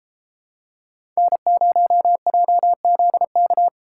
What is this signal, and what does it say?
Morse code, heard as a keyed tone, about 32 wpm, 700 Hz. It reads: N0JZK